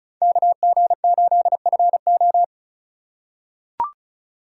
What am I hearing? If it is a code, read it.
Morse code, heard as a keyed tone, about 36 wpm, 690 Hz. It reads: KG8FO
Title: USA Call Signs - 5 Characters - 3x Character Spacing - 35wpm